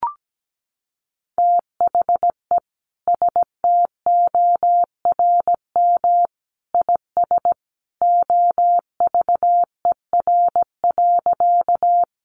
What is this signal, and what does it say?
Morse code, heard as a keyed tone, about 17 wpm, 700 Hz. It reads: THE STORM IS OVER.
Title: Sentences from Top 1500 Words 17wpm